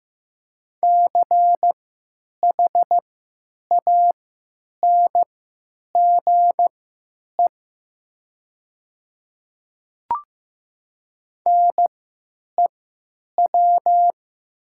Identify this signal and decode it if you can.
Morse code, heard as a keyed tone, about 15 wpm, spaced as for 8 wpm, 700 Hz. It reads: CHANGE NEW
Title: 3x Character Spacing - Top 200 Words - 15wpm